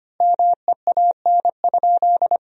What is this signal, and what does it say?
Morse code, heard as a keyed tone, about 25 wpm, 700 Hz. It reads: MEAN?
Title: 2x Word-Spacing - Sentences from Top 200 Words - 25wpm